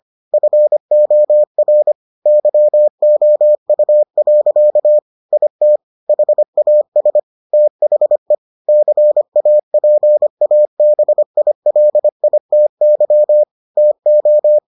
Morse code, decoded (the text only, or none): FOR YOU. IT HAS THE CAPABILITY TO